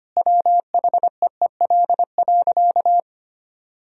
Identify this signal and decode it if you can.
Morse code, heard as a keyed tone, about 25 wpm, 700 Hz. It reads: WHEEL.
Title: Sentences from Top 1000 Words - Review Part 3 - 25wpm